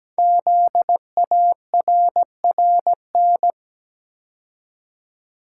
Morse code, heard as a keyed tone, about 17 wpm, 700 Hz. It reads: ZARRN
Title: Code Groups - 5 Letters - 17wpm